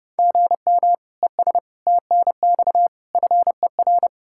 Morse code, heard as a keyed tone, about 36 wpm, 700 Hz. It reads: GM ES TNX FER
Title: Speed-Racing Common QSO Phrases - 30wpm